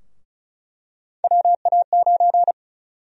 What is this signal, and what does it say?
Morse code, heard as a keyed tone, about 34 wpm, 700 Hz. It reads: WA9
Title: USA Call Sign Prefixes - Repeated 2x - 35wpm